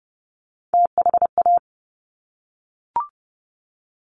Morse code, heard as a keyed tone, about 30 wpm, 700 Hz. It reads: THA
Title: Top Three Letter Combinations in Words 30wpm